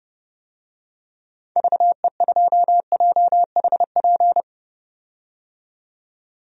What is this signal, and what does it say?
Morse code, heard as a keyed tone, about 32 wpm, 700 Hz. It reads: VE2JHP
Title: DXCC-100 Call Signs - 6 Characters - 30wpm